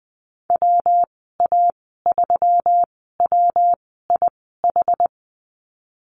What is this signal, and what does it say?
Morse code, heard as a keyed tone, about 20 wpm, spaced as for 15 wpm, 700 Hz. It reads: WA3WIH
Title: 179 Lesson 29g - Slash - Call Signs 20wpm